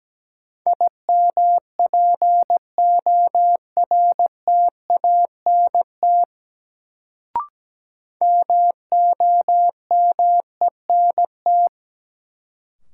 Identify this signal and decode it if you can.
Morse code, heard as a keyed tone, about 17 wpm, 700 Hz. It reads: IMPORTANT MOMENT
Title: Sets of 3 Words - New Word Drill - 17wpm